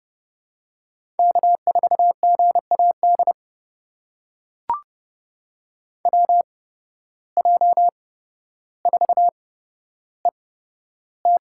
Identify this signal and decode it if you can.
Morse code, heard as a keyed tone, about 30 wpm, spaced as for 13 wpm, 700 Hz. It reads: K4GAD WJ4ET
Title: USA Call Signs - 5 Characters - 8x Character Spacing - 30wpm